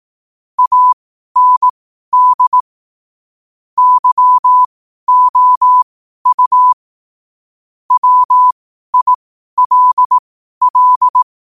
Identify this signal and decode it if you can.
Morse code, heard as a keyed tone, about 18 wpm, spaced as for 12 wpm, 1000 Hz. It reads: AND YOU WILL
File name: qotdfiles / 12wpm